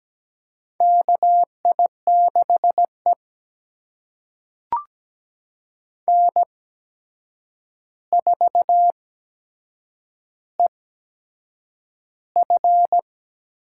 Morse code, heard as a keyed tone, about 17 wpm, spaced as for 7 wpm, 700 Hz. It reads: KI6E N4EF
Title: USA Call Signs - 4 Characters - 8x Character Spacing - 17wpm